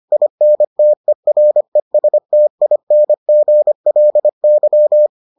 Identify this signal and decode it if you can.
Morse code, heard as a keyed tone, about 25 wpm, 600 Hz. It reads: INTERESTINGLY